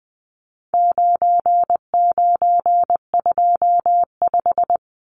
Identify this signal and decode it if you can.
Morse code, heard as a keyed tone, about 20 wpm, 700 Hz. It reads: 9925
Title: Numbers - 4-Digit 20wpm